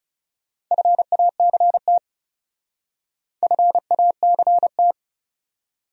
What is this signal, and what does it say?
Morse code, heard as a keyed tone, about 34 wpm, 700 Hz. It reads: FACT FACT
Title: Speed-Racing Top 500 Words 30wpm